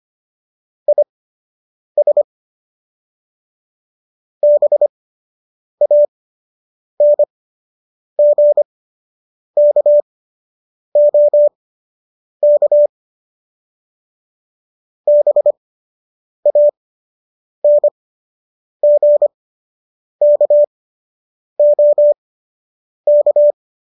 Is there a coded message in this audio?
Yes: Morse code from 0.9 to 23.5 s, heard as a keyed tone, about 25 wpm, spaced as for 8 wpm, 590 Hz. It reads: IS BANGKOK BANGKOK